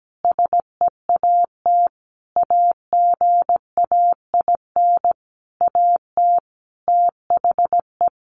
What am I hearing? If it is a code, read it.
Morse code, heard as a keyed tone, about 17 wpm, 700 Hz. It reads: SEAT AGAIN AT THE
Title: Sentences from Top 1500 Words 17wpm